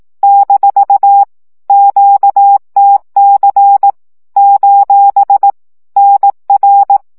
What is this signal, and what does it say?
Morse code, heard as a keyed tone, about 18 wpm, 800 Hz. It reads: - QTC : NR
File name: morze vizsga 2011-3 15